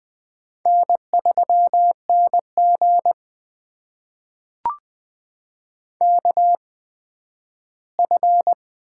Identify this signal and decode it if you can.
Morse code, heard as a keyed tone, about 20 wpm, spaced as for 12 wpm, 700 Hz. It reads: N3NG KF
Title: USA Call Signs - 8x Character Spacing - 20wpm